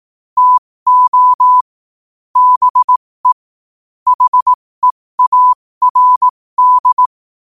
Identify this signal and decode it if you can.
Morse code, heard as a keyed tone, about 18 wpm, spaced as for 15 wpm, 1000 Hz. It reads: TO BE HEARD
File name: qotdfiles / 15wpm